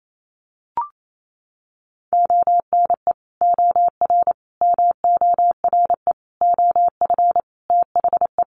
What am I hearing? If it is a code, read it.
Morse code, heard as a keyed tone, about 28 wpm, 700 Hz. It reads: ONE OR MORE OF THE
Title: Sets of 5 Words Encore 28wpm